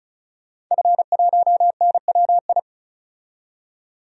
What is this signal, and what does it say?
Morse code, heard as a keyed tone, about 36 wpm, 700 Hz. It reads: F1NWI
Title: DXCC-100 Call Signs - 5 Characters - 35wpm